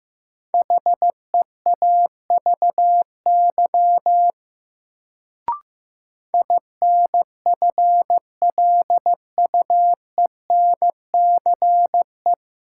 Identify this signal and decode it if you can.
Morse code, heard as a keyed tone, about 15 wpm, 700 Hz. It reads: HEAVY INFLUENCE
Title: Sentences from Top 1100 Words - New Word Drill Alt - 15wpm